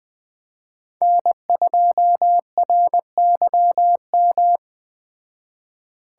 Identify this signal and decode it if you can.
Morse code, heard as a keyed tone, about 20 wpm, 700 Hz. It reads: N2RYM